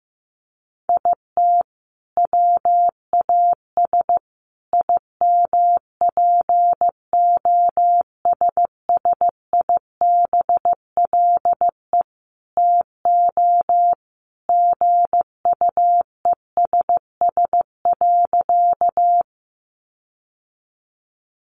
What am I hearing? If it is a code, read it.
Morse code, heard as a keyed tone, about 15 wpm, 700 Hz. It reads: IT WAS IMPOSSIBLE TO GUESS.
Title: Sentences from Top 1500 Words 15wpm